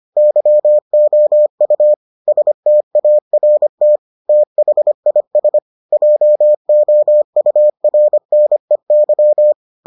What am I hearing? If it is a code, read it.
Morse code, heard as a keyed tone, about 25 wpm, 600 Hz. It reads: YOU START THIS JOURNEY